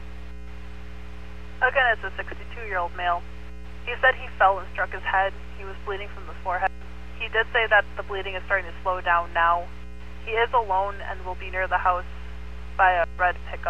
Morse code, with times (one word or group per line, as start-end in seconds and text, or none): none